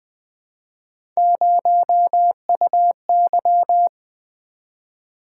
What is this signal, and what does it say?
Morse code, heard as a keyed tone, about 20 wpm, 700 Hz. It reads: 0UY
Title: Three Letter-Number Combinations - 20wpm